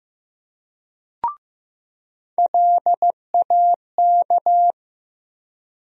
Morse code, heard as a keyed tone, about 15 wpm, 700 Hz. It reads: LAK
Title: USA Call Sign Postfixes - 3 Letters 15wpm